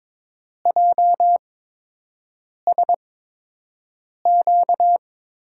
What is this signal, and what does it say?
Morse code, heard as a keyed tone, about 22 wpm, spaced as for 6 wpm, 700 Hz. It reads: JSQ
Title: USA Call Signs - 6 Characters - 8x Character Spacing - 22wpm